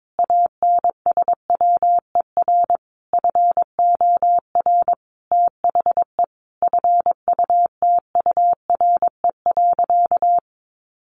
Morse code, heard as a keyed tone, about 22 wpm, 700 Hz. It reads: ANSWER FOR THE FUTURE.
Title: Sentences from Top 1400 Words 22wpm